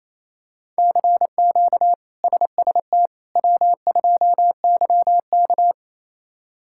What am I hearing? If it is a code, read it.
Morse code, heard as a keyed tone, about 28 wpm, 700 Hz. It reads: CQ SST W2YK
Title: K1USN SST QSO Practice - 28wpm